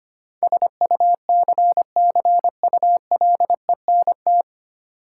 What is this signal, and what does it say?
Morse code, heard as a keyed tone, about 25 wpm, 700 Hz. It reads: SUCCULENT
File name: practice / 25wpm / Difficult Words with Hints - 25wpm